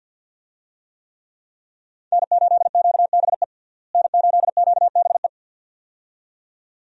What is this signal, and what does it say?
Morse code, heard as a keyed tone, about 48 wpm, 700 Hz. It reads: N8QBE N8QBE
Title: USA Call Signs - 5 Characters - Repeated 2x - 50wpm